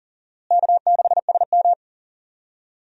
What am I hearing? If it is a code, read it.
Morse code, heard as a keyed tone, about 40 wpm, 700 Hz. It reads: K6SM
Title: USA Call Signs - 4 Characters - Repeated 2x - 40wpm